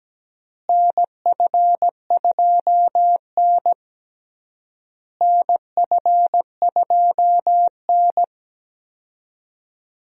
Morse code, heard as a keyed tone, about 17 wpm, 700 Hz. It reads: NF2N NF2N
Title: USA Call Signs - 4 Characters - Repeated 3x - 17wpm